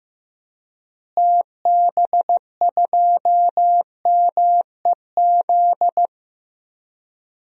Morse code, heard as a keyed tone, about 15 wpm, 700 Hz. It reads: TB2MEZ
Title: DXCC-100 Call Signs - 6 Characters - Repeated 3x - 15wpm